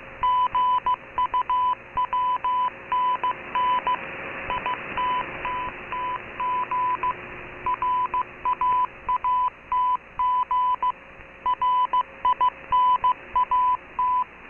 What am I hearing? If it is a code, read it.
Morse code, heard as a keyed tone, about 15 wpm, 1000 Hz. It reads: GUWNN ITTTG RAATG RINAT